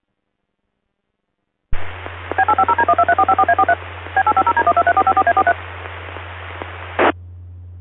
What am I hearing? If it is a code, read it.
DTMF, heard as telephone touch tones, tones of about 70 ms. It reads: B850D123454A43 B850D123454A43
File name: hamrepeaters / zavar / HG9RVC